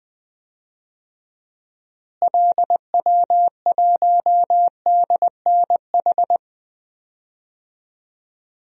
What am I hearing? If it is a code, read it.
Morse code, heard as a keyed tone, about 20 wpm, 700 Hz. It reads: LW1DNH
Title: DXCC-100 Call Signs - 6 Characters - Repeated 2x - 20wpm